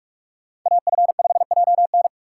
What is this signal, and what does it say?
Morse code, heard as a keyed tone, about 46 wpm, 700 Hz. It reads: AF5JN
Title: USA Call Signs - 5 Characters - 6x Character Spacing - 45wpm